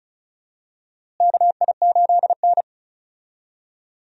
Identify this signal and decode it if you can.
Morse code, heard as a keyed tone, about 34 wpm, 700 Hz. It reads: KI8N
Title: USA Call Signs - 4 Characters - Repeated 3x - 35wpm